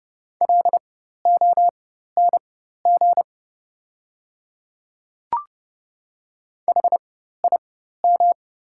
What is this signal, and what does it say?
Morse code, heard as a keyed tone, about 30 wpm, spaced as for 14 wpm, 700 Hz. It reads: LONG HIM
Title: 4x Character Spacing - Top 100 Words - 30wpm